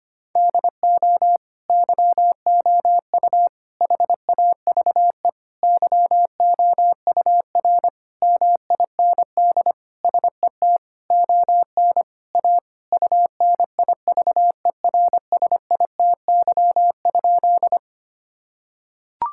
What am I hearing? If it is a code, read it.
Morse code, heard as a keyed tone, about 25 wpm, 700 Hz. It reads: DO YOU HAVE YOUR MIND SET ON A UNIVERSITY?
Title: Sentences from Top 1000 Words - Review Part 5 - 25wpm